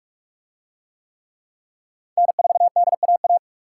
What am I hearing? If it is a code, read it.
Morse code, heard as a keyed tone, about 48 wpm, 700 Hz. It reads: N4DAA